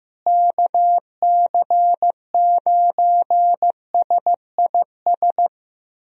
Morse code, heard as a keyed tone, about 16 wpm, 700 Hz. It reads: KC9SIS
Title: USA Call Signs - 6 Characters - Repeated 2x - 15wpm